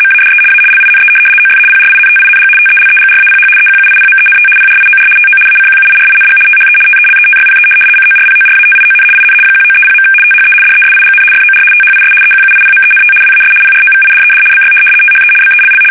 RTTY, about 75 baud. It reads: MADE A LINE OF PRINTING TELEGRAPH EQUIPMENT PRIOR TO ACQUIRING THE TELETYPE CORPORATION IN 1930. THE DESIGN FOR THIS EQUIPMENT WAS PROVIDED BY THE BELL
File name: fsk850-75B / teletype-corp